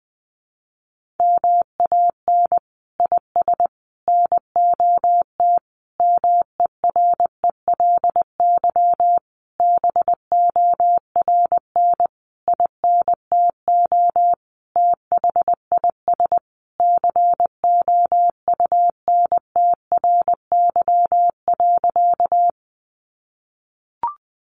Morse code, heard as a keyed tone, about 20 wpm, 700 Hz. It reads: MAN IS NOT MERELY BORN INTO THIS COUNTRY.
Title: Sentences from Top 1400 Words 20wpm